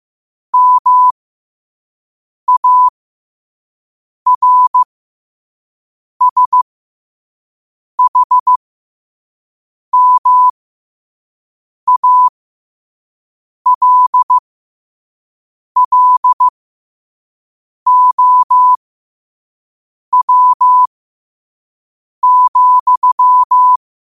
Morse code, heard as a keyed tone, about 15 wpm, spaced as for 5 wpm, 1000 Hz. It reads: MARSHMALLOW,
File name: qotdfiles / 5wpm